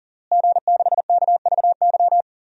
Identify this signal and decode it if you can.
Morse code, heard as a keyed tone, about 40 wpm, 700 Hz. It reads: G6KVY
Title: DXCC-100 Call Signs - 5 Characters - Repeated 2x - 40wpm